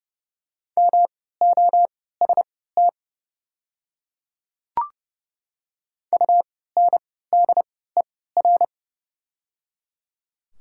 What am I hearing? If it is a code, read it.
Morse code, heard as a keyed tone, about 30 wpm, spaced as for 17 wpm, 690 Hz. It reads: MOST UNDER
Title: 3x Character Spacing - Top 200 Words - 30wpm